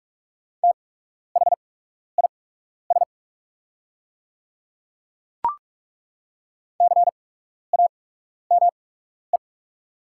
Morse code, heard as a keyed tone, about 43 wpm, spaced as for 12 wpm, 690 Hz. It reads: THIS CAME
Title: 8x Character Spacing - Top 500 Words - 45wpm